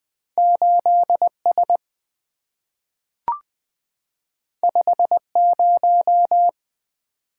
Morse code, heard as a keyed tone, about 20 wpm, 700 Hz. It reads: 8S 50
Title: Two Letter-Number Combinations - 20wpm